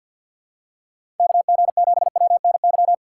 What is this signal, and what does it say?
Morse code, heard as a keyed tone, about 51 wpm, 700 Hz. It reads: KG7WNY
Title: USA Call Signs - 6 Characters - 6x Character Spacing - 50wpm